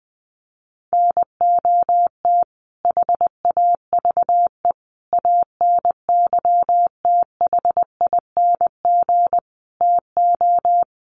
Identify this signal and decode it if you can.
Morse code, heard as a keyed tone, about 20 wpm, 700 Hz. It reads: NOT HAVE ANYTHING TO
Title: Sets of 5 Words Encore 20wpm